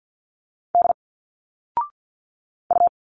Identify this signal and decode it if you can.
Morse code, heard as a keyed tone, about 51 wpm, 700 Hz. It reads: D U
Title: Single Letters 50wpm